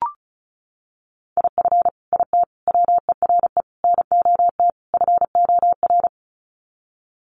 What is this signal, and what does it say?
Morse code, heard as a keyed tone, about 34 wpm, 700 Hz. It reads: IF IT WERE NOT FOR
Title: Sets of 5 Words 35wpm